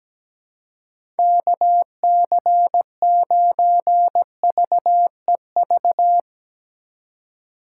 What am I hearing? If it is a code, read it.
Morse code, heard as a keyed tone, about 17 wpm, 700 Hz. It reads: KC9VEV
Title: USA Call Signs - 6 Characters - 2x Character Spacing - 17wpm